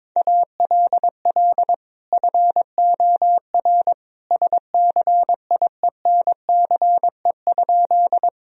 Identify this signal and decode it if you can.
Morse code, heard as a keyed tone, about 22 wpm, 700 Hz. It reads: ALL FOR SCIENCE?
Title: Sentences from Top 1000 Words - Review Part 2 - 22wpm